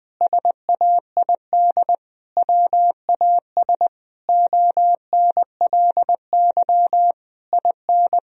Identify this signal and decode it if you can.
Morse code, heard as a keyed tone, about 20 wpm, 700 Hz. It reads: SAID WAS ONLY IN
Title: Sentences from Top 1000 Words - Review Part 4 - 20wpm